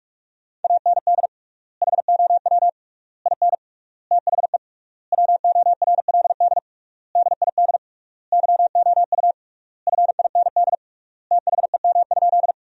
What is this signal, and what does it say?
Morse code, heard as a keyed tone, about 41 wpm, 700 Hz. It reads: AND HOW IN THE WORLD DID YOU FIND THEM?
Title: 3x Word-Spacing - Sentences from Top 200 Words - 45wpm